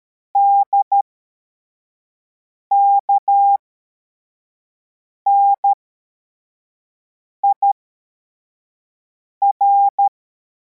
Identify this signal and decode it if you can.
Morse code, heard as a keyed tone, about 13 wpm, spaced as for 4 wpm, 800 Hz. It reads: DKNIR